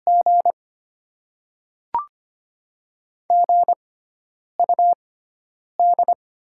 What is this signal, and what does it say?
Morse code, heard as a keyed tone, about 25 wpm, spaced as for 9 wpm, 700 Hz. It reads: G GUD
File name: Top 20 QSO Elements - 6x Character Spacing - 25wpm